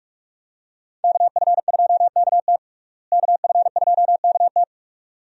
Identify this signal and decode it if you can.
Morse code, heard as a keyed tone, about 44 wpm, 700 Hz. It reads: KF2KT KF2KT
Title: USA Call Signs - 5 Characters - Repeated 2x - 45wpm